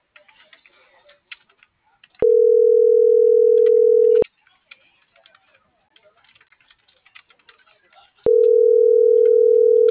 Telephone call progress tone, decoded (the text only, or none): ringback tone